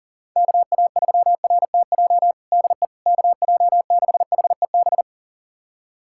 Morse code, heard as a keyed tone, about 40 wpm, 700 Hz. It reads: KA3RTJ DE KJ6HEB
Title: USA Callsign DE Callsign 40wpm